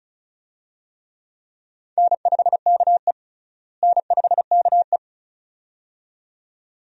Morse code, heard as a keyed tone, about 36 wpm, 700 Hz. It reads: N5KE N5KE
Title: USA Call Signs - 4 Characters - Repeated 2x - 35wpm